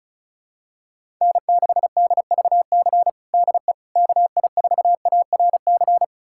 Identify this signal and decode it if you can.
Morse code, heard as a keyed tone, about 36 wpm, 700 Hz. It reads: N6DVC DE KI4ARC
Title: USA Callsign DE Callsign 35wpm